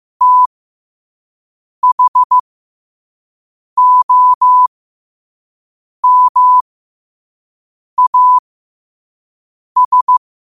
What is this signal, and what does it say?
Morse code, heard as a keyed tone, about 15 wpm, spaced as for 5 wpm, 1000 Hz. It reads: THOMAS
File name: qotdfiles / 5wpm